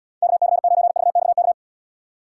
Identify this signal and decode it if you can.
Morse code, heard as a keyed tone, about 77 wpm, 700 Hz. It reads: KC9VXK